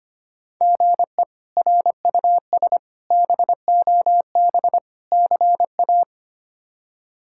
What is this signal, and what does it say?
Morse code, heard as a keyed tone, about 25 wpm, 700 Hz. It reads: GE RUS BOB CA
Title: K1USN SST QSO Practice - 25wpm